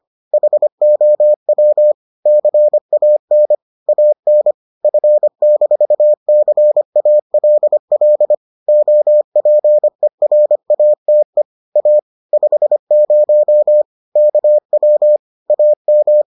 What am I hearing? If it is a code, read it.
Morse code, heard as a keyed tone, about 25 wpm, 600 Hz. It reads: HOW CAN AN F-CALL OPERATE A 50 KW AM